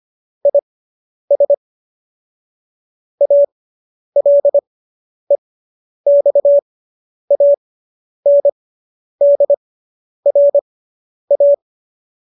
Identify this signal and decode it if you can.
Morse code, heard as a keyed tone, about 25 wpm, spaced as for 10 wpm, 570 Hz. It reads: IS ALEXANDRA